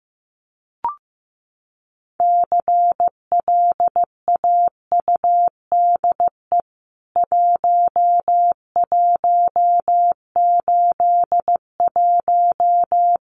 Morse code, heard as a keyed tone, about 15 wpm, 700 Hz. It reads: CLAUDE 1181
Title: CWT Exchange 15wpm